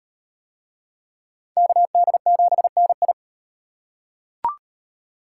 Morse code, heard as a keyed tone, about 36 wpm, 700 Hz. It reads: KD7NI